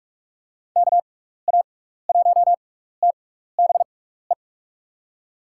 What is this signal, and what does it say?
Morse code, heard as a keyed tone, about 46 wpm, spaced as for 16 wpm, 700 Hz. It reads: KA1TBE
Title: USA Call Signs - 6 Characters - 6x Character Spacing - 45wpm